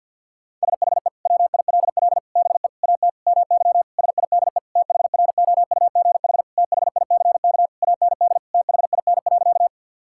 Morse code, heard as a keyed tone, about 50 wpm, 700 Hz. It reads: SHE WILL BE AT MY SIDE THROUGH THICK AND THIN.
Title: Sentences from Top 1600 Words 50wpm